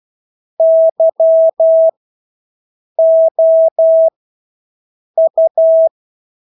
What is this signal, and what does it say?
Morse code, heard as a keyed tone, about 12 wpm, spaced as for 6 wpm, 650 Hz. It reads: YOU
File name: Engelske ord 12-6 wpm